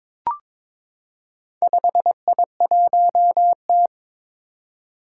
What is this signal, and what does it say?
Morse code, heard as a keyed tone, about 22 wpm, 700 Hz. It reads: 5I1T
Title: Four Challenging Letter-Number Combinations 22wpm